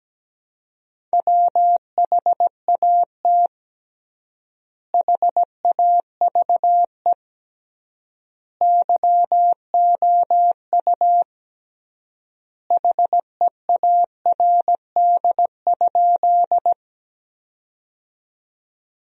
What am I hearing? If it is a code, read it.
Morse code, heard as a keyed tone, about 17 wpm, 700 Hz. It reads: WHAT HAVE YOU HEARD?
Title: 3x Word-Spacing - Sentences from Top 400 Words - 17wpm